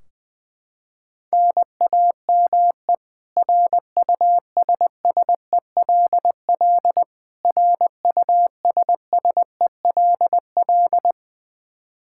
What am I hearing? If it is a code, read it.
Morse code, heard as a keyed tone, about 21 wpm, 700 Hz. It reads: NAME RUSSELL RUSSELL
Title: K1USN SST QSO Practice 20wpm